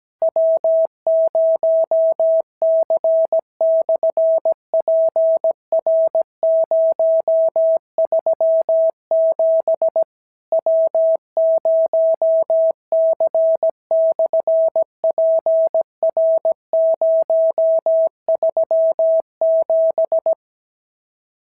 Morse code, heard as a keyed tone, about 17 wpm, 650 Hz. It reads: W0C/PR037 W0C/PR037
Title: SOTA QSO Practice 17wpm